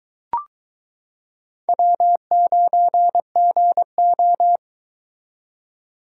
Morse code, heard as a keyed tone, about 22 wpm, 700 Hz. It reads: W9GO